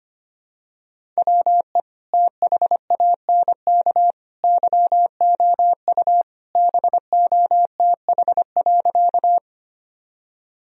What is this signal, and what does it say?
Morse code, heard as a keyed tone, about 25 wpm, 700 Hz. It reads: WE THANK YOU BOTH.
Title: Sentences from Top 1000 Words - Review Part 1 - 25wpm